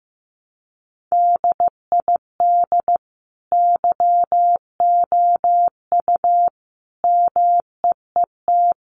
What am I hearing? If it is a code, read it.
Morse code, heard as a keyed tone, about 15 wpm, 700 Hz. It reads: DID YOU MEET
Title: Sentences from Top 1100 Words 15wpm